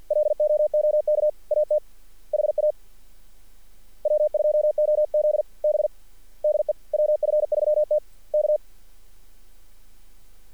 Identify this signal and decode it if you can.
Morse code, heard as a keyed tone, about 48 wpm, 600 Hz. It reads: LOOK AT HA W2OZB DE WF3T K
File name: archived / CRSnet